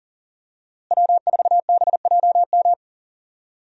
Morse code, heard as a keyed tone, about 37 wpm, 700 Hz. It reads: W4BJM
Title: USA Call Signs - 5 Characters - Repeated 3x - 40wpm